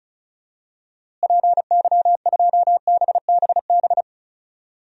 Morse code, heard as a keyed tone, about 36 wpm, 700 Hz. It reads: PY2BBB
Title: DXCC-100 Call Signs - 6 Characters - Repeated 2x - 35wpm